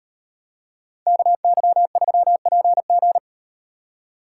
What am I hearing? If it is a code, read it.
Morse code, heard as a keyed tone, about 38 wpm, 700 Hz. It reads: KY3PG